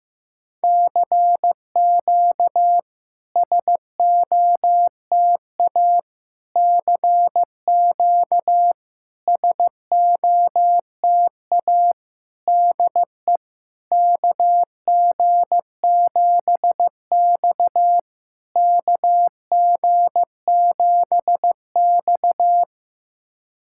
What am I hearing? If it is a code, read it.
Morse code, heard as a keyed tone, about 16 wpm, 700 Hz. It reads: CQ SOTA CQ SOTA DE KG7X KG7X
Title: SOTA QSO Practice 15wpm